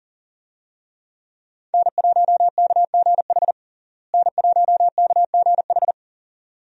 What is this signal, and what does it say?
Morse code, heard as a keyed tone, about 40 wpm, 700 Hz. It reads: N1KGH N1KGH
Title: USA Call Signs - 5 Characters - Repeated 2x - 40wpm